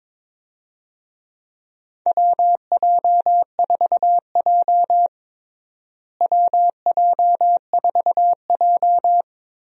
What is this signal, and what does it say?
Morse code, heard as a keyed tone, about 22 wpm, 700 Hz. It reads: WJ4J WJ4J
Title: USA Call Signs - 4 Characters - Repeated 3x - 22wpm